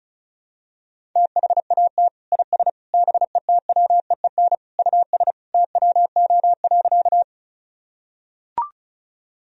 Morse code, heard as a keyed tone, about 36 wpm, 700 Hz. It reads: THAT IS BETWEEN US TWO.
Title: Sentences from Top 1000 Words - Review Part 1 - 35wpm